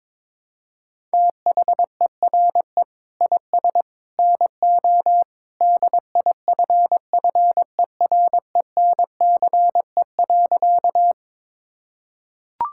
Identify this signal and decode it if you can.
Morse code, heard as a keyed tone, about 22 wpm, 700 Hz. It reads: THERE IS NO DIFFERENCE.
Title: Sentences from Top 1000 Words - Review Part 5 - 22wpm